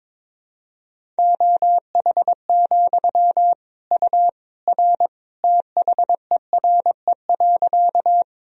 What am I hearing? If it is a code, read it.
Morse code, heard as a keyed tone, about 22 wpm, 700 Hz. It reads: OH, U R THERE.
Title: Sentences from Top 500 Words and Abbreviations 22wpm